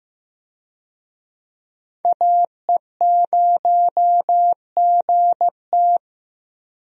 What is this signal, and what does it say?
Morse code, heard as a keyed tone, about 15 wpm, 700 Hz. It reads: AE0GT
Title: USA Call Signs - 5 Characters - Repeated 2x - 15wpm